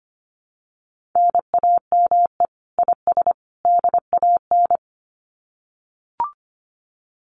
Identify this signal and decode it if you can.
Morse code, heard as a keyed tone, about 25 wpm, 700 Hz. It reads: NAME IS DAN
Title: Common QSO Phrases 25wpm